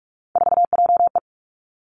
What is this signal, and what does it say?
Morse code, heard as a keyed tone, about 46 wpm, 700 Hz. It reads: 4JE
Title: Three Challenging Letter-Number Combinations 45wpm